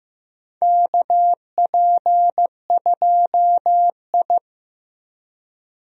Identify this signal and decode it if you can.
Morse code, heard as a keyed tone, about 15 wpm, 700 Hz. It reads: KP2I
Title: DXCC-100 Call Signs - 4 Characters - 15wpm